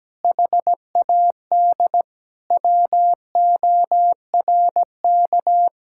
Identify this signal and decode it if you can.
Morse code, heard as a keyed tone, about 17 wpm, 700 Hz. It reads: HAD WORK
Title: Sentences from Top 1600 Words 17wpm